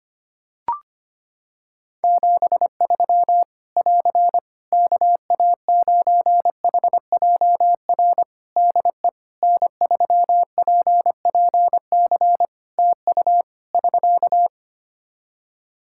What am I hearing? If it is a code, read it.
Morse code, heard as a keyed tone, about 25 wpm, 700 Hz. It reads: 73 <AR> KA9HJR DE N3PPC TU <SK>
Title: Practice QSO - Exchange 4 - Ending - 25wpm